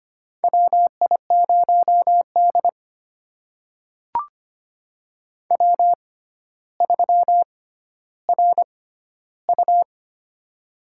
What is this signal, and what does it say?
Morse code, heard as a keyed tone, about 25 wpm, spaced as for 13 wpm, 700 Hz. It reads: WI0D W3RU
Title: USA Call Signs - 4 Characters - 6x Character Spacing - 25wpm